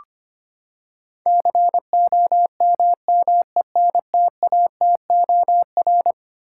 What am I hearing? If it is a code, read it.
Morse code, heard as a keyed tone, about 25 wpm, 700 Hz. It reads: COMMENTATOR
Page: morse-code-ninja-practice-25wpm